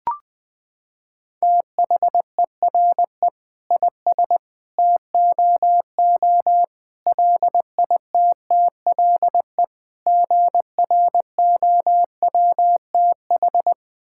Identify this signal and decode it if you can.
Morse code, heard as a keyed tone, about 20 wpm, 700 Hz. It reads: THERE IS TOO LITTLE GROWTH
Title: Sentences from Top 1000 Words - Review Part 5 - 20wpm